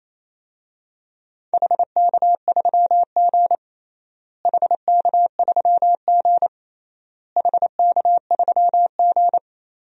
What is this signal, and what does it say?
Morse code, heard as a keyed tone, about 28 wpm, 700 Hz. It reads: HK3G HK3G HK3G
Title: DXCC-100 Call Signs - 4 Characters - Repeated 3x - 28wpm